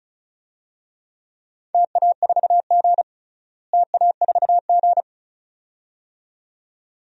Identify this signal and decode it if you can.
Morse code, heard as a keyed tone, about 34 wpm, 700 Hz. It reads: TA4G TA4G
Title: DXCC-100 Call Signs - 4 Characters - Repeated 2x - 35wpm